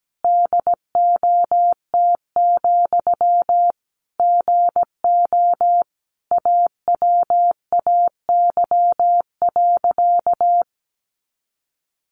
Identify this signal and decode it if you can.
Morse code, heard as a keyed tone, about 17 wpm, 700 Hz. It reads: DOT, GO AWAY.
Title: Sentences from Top 1500 Words 17wpm